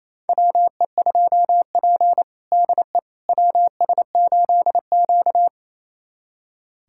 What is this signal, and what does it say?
Morse code, heard as a keyed tone, about 28 wpm, 700 Hz. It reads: WE2P DE WS8Q
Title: USA Callsign DE Callsign - 4 Characters - 28wpm